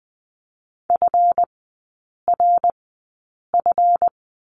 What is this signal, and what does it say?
Morse code, heard as a keyed tone, about 20 wpm, spaced as for 8 wpm, 700 Hz. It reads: FRF
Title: 138 Lesson 23e - Letter f - Three Characters 20wpm